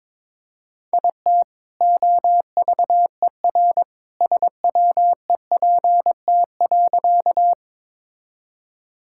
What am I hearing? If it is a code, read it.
Morse code, heard as a keyed tone, about 22 wpm, 700 Hz. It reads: IT OVER SWEPT.
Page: morse-code-ninja-practice-22wpm